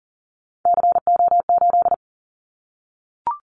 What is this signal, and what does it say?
Morse code, heard as a keyed tone, about 40 wpm, 700 Hz. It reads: CO8
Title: Callsign Prefixes for Top 100 DXCC - 40wpm